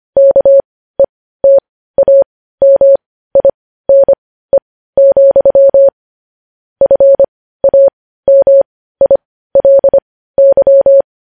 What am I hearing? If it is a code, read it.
Morse code, heard as a keyed tone, about 25 wpm, spaced as for 15 wpm, 550 Hz. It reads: KETAMINE, FAMILY
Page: news headlines in morse code at 15 wpm